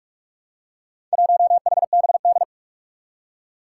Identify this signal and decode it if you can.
Morse code, heard as a keyed tone, about 44 wpm, 700 Hz. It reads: 1HBD
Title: Four Challenging Letter-Number Combinations 45wpm